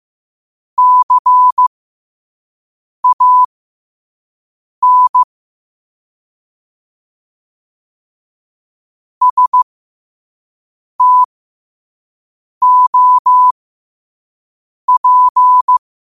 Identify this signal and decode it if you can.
Morse code, heard as a keyed tone, about 15 wpm, spaced as for 5 wpm, 1000 Hz. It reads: CAN STOP
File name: qotdfiles / 5wpm